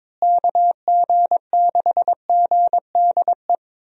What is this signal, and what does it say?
Morse code, heard as a keyed tone, about 22 wpm, 700 Hz. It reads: KG6GDE
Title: USA Call Signs - 6 Characters - Repeated 2x - 22wpm